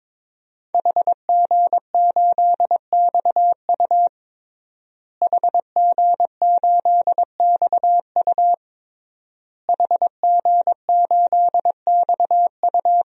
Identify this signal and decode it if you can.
Morse code, heard as a keyed tone, about 22 wpm, 700 Hz. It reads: HG8XU HG8XU HG8XU
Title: DXCC-100 Call Signs - 5 Characters - Repeated 3x - 22wpm